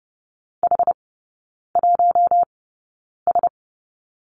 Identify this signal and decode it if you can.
Morse code, heard as a keyed tone, about 30 wpm, spaced as for 9 wpm, 700 Hz. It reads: H1S
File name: 081 Lesson 14e - Number 2 - Three Characters 30wpm